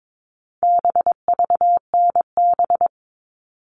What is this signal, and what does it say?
Morse code, heard as a keyed tone, about 22 wpm, 700 Hz. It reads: BVNB
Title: All Cut Numbers - 4-Digit 22wpm